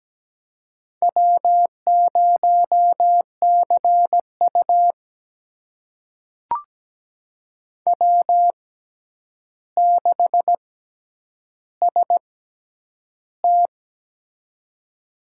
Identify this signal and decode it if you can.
Morse code, heard as a keyed tone, about 17 wpm, spaced as for 9 wpm, 700 Hz. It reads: W0CU W6ST
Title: USA Call Signs - 4 Characters - 6x Character Spacing - 17wpm